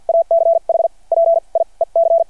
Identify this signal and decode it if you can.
Morse code, heard as a keyed tone, about 53 wpm, spaced as for 38 wpm, 670 Hz. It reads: ACH PIEC
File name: archived / CRSnet